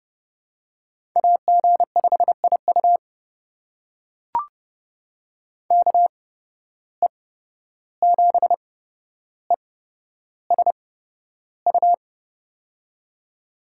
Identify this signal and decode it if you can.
Morse code, heard as a keyed tone, about 29 wpm, spaced as for 12 wpm, 700 Hz. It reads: AG5IU KE7ESU
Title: USA Call Signs - 8x Character Spacing - 30wpm